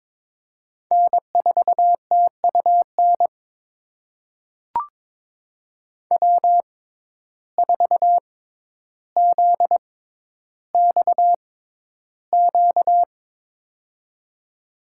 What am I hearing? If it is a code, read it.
Morse code, heard as a keyed tone, about 22 wpm, spaced as for 11 wpm, 700 Hz. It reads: N4TUN W4ZXQ
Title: USA Call Signs - 6x Character Spacing - 22wpm